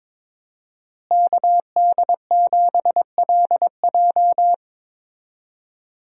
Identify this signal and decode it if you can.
Morse code, heard as a keyed tone, about 22 wpm, 700 Hz. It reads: KD7LJ
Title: USA Call Signs - 5 Characters - Repeated 3x - 22wpm